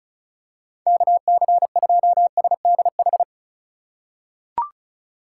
Morse code, heard as a keyed tone, about 36 wpm, 700 Hz. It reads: KC2SDH
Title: USA Call Signs - 6 Characters - 3x Character Spacing - 35wpm